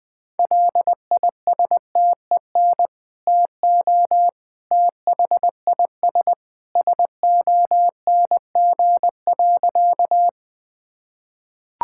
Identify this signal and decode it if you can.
Morse code, heard as a keyed tone, about 20 wpm, 700 Hz. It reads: LISTEN TO THIS SONG.
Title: Sentences with 2-Word Phrases - 20wpm